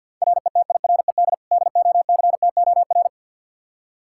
Morse code, heard as a keyed tone, about 47 wpm, 700 Hz. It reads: RETIRED DOCTOR
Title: Practice QSO - Exchange 3 - More Info - 50wpm